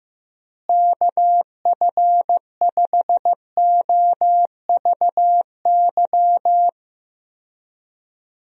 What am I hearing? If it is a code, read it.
Morse code, heard as a keyed tone, about 15 wpm, 700 Hz. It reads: KF5OVY
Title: USA Call Signs - 6 Characters - Repeated 2x - 15wpm